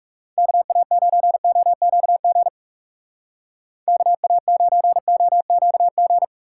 Morse code, heard as a keyed tone, about 36 wpm, 700 Hz. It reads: KA9OQG KA9OQG
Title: Speed Racing - USA 2x3 Call Signs 30wpm